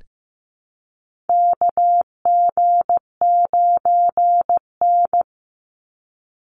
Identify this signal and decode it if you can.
Morse code, heard as a keyed tone, about 15 wpm, 700 Hz. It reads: KG9N
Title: Call Signs Repeated 3x - 15wpm